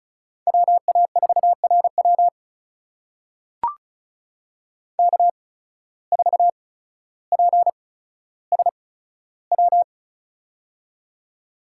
Morse code, heard as a keyed tone, about 36 wpm, spaced as for 15 wpm, 700 Hz. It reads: WA4RW K4PSW
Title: USA Call Signs - 5 Characters - 8x Character Spacing - 35wpm